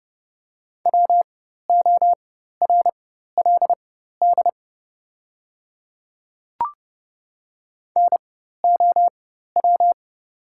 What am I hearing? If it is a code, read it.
Morse code, heard as a keyed tone, about 30 wpm, spaced as for 14 wpm, 700 Hz. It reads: WORLD NOW
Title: 4x Character Spacing - Top 300 Words - 30wpm